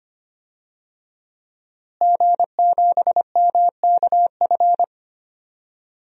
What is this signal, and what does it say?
Morse code, heard as a keyed tone, about 25 wpm, 700 Hz. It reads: G7MKF
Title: DXCC-100 Call Signs - 5 Characters - Repeated 2x - 25wpm